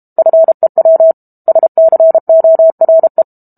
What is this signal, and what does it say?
Morse code, heard as a keyed tone, about 32 wpm, 650 Hz. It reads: FEW SCORE
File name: Book Two / 06-Chapter 6 - 32 WPM